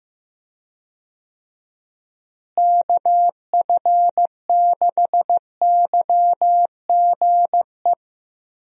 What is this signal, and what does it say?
Morse code, heard as a keyed tone, about 15 wpm, 700 Hz. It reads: KF6YGE
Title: USA Call Signs - 6 Characters - Repeated 2x - 15wpm